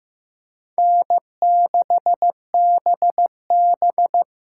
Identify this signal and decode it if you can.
Morse code, heard as a keyed tone, about 18 wpm, 700 Hz. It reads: N6BB
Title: Speed Racing - USA 1x2 Call Signs 15wpm